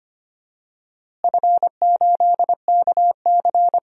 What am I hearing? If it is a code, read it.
Morse code, heard as a keyed tone, about 25 wpm, 700 Hz. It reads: F8KC
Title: DXCC-100 Call Signs - 4 Characters - Repeated 3x - 25wpm